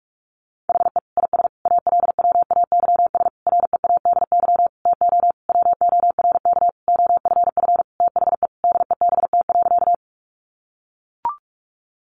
Morse code, heard as a keyed tone, about 44 wpm, 700 Hz. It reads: HE IS ALWAYS READY TO WORK OFF THE DEBT.
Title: Sentences from Top 1300 Words 45wpm